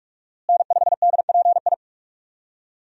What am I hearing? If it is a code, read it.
Morse code, heard as a keyed tone, about 46 wpm, 700 Hz. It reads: N5DPI
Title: USA Call Signs - 5 Characters - Repeated 3x - 45wpm